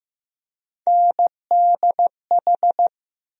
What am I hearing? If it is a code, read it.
Morse code, heard as a keyed tone, about 15 wpm, 700 Hz. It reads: NDH
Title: USA Call Sign Postfixes - 3 Letters 15wpm